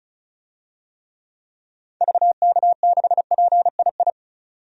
Morse code, heard as a keyed tone, about 36 wpm, 700 Hz. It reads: VK6PII
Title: DXCC-100 Call Signs - 6 Characters - 35wpm